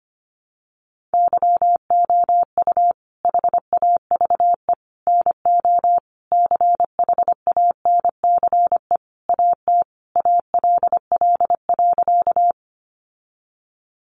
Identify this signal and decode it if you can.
Morse code, heard as a keyed tone, about 25 wpm, 700 Hz. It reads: YOU HAVE NO CHANCE AT ALL.
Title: Sentences from Top 800 Words 25wpm